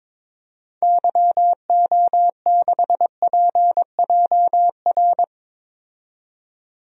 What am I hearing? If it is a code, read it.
Morse code, heard as a keyed tone, about 22 wpm, 700 Hz. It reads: YO6PJR
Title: DXCC-100 Call Signs - 6 Characters - 22wpm